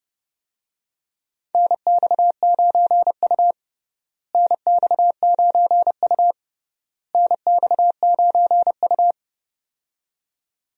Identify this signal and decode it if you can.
Morse code, heard as a keyed tone, about 31 wpm, 700 Hz. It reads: NX9U NX9U NX9U
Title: USA Call Signs - 4 Characters - Repeated 3x - 30wpm